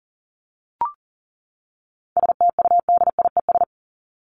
Morse code, heard as a keyed tone, about 40 wpm, 700 Hz. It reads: STUDIES
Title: ES:S Suffix - Top 100 Words with Base Words 40wpm